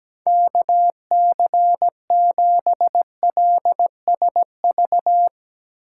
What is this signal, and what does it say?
Morse code, heard as a keyed tone, about 17 wpm, 700 Hz. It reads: KC7LSV
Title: USA Call Signs - 4x Character Spacing - 17wpm